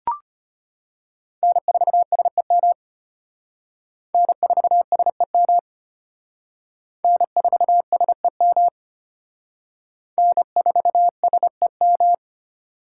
Morse code, heard as a keyed tone, about 30 wpm, 700 Hz. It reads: N4SEM N4SEM N4SEM N4SEM
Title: Speed Racing - USA 1x3 Call Signs 25wpm